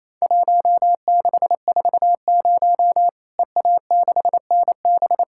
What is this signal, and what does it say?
Morse code, heard as a keyed tone, about 28 wpm, 700 Hz. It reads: 1640 EA6NB
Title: Worked All Europe DX Contest - QTC Exchange - 28wpm